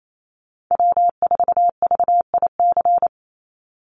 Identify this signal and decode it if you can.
Morse code, heard as a keyed tone, about 28 wpm, 700 Hz. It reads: W4VIC
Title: Call Signs Repeated 3x - 28wpm